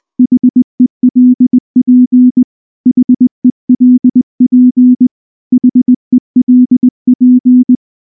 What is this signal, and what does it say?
Morse code, heard as a keyed tone, about 20 wpm, 250 Hz. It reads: HELP HELP HELP
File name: morse / morse-training-20WPM-4